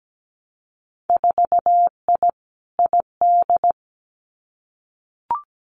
Morse code, heard as a keyed tone, about 17 wpm, 700 Hz. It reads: 4I ID
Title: Winter Field Day Exchanges 17wpm